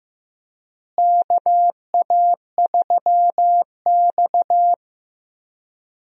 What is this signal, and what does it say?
Morse code, heard as a keyed tone, about 15 wpm, 700 Hz. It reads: KA3X
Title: USA Call Signs - 4 Characters - Repeated 3x - 15wpm